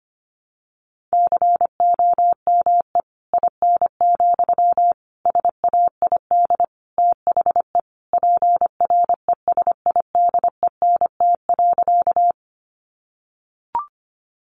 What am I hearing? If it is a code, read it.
Morse code, heard as a keyed tone, about 25 wpm, 700 Hz. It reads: COME IN, SAID THE PRESIDENT.
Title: Sentences from Top 1000 Words 25wpm